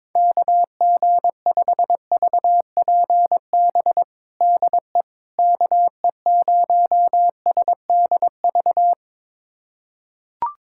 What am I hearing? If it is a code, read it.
Morse code, heard as a keyed tone, about 22 wpm, 700 Hz. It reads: KG5VPB DE KE0SDV
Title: USA Callsign DE Callsign 22wpm